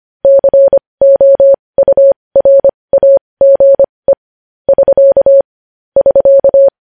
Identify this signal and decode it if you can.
Morse code, heard as a keyed tone, about 25 wpm, spaced as for 20 wpm, 550 Hz. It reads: COURAGE <SK> <SK>
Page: news-headlines-in-morse-code-at-20-wpm